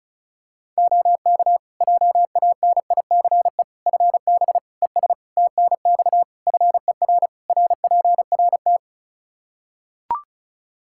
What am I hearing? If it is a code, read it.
Morse code, heard as a keyed tone, about 36 wpm, 700 Hz. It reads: OK JANICE FB ES TNX FER RPRT
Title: K1USN SST QSO Practice 35wpm